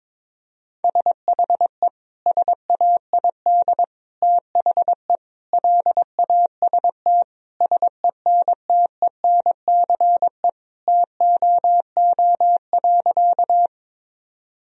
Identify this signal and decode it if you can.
Morse code, heard as a keyed tone, about 22 wpm, 700 Hz. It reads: SHE SAID THE LAST SENTENCE TOO.
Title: Sentences from Top 1000 Words - Review Part 1 - 22wpm